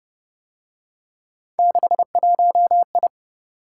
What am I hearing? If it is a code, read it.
Morse code, heard as a keyed tone, about 31 wpm, 700 Hz. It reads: 61I